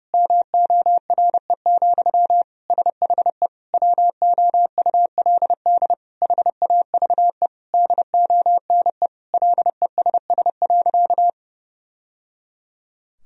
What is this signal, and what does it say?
Morse code, heard as a keyed tone, about 32 wpm, 700 Hz. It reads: MORE, SHE WOULD HAVE DONE LESS.
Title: Sentences from Top 1000 Words - Review Part 3 - 30wpm